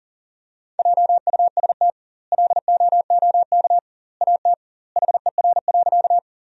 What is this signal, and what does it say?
Morse code, heard as a keyed tone, about 37 wpm, 700 Hz. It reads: JUST LOOK AT HER.
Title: 2x Word-Spacing - Sentences from Top 200 Words - 40wpm